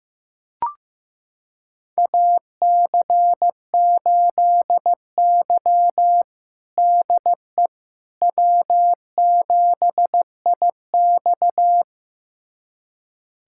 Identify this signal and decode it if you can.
Morse code, heard as a keyed tone, about 16 wpm, 700 Hz. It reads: AC8Y DE W7IX
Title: USA Callsign DE Callsign - 4 Characters - 15wpm